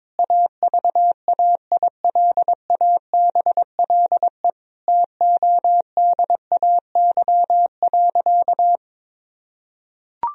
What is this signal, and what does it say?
Morse code, heard as a keyed tone, about 22 wpm, 700 Hz. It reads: AVAILABLE TODAY.
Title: Sentences from Top 1000 Words - Review Part 3 - 22wpm